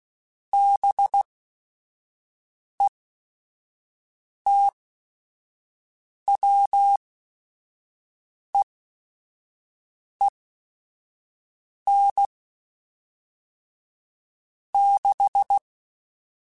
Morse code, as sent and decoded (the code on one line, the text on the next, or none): -... - .-- . -. / -....
BETWEEN 6